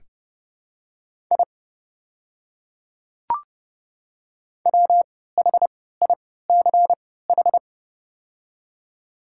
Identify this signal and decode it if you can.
Morse code, heard as a keyed tone, about 30 wpm, spaced as for 17 wpm, 690 Hz. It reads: I WHICH
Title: 3x Character Spacing - Top 100 Words - 30wpm